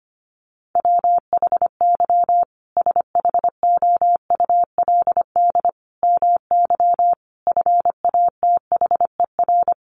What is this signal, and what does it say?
Morse code, heard as a keyed tone, about 25 wpm, 700 Hz. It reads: WHY SHOULD MY FATHER
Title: Sentences from Top 500 Words 25wpm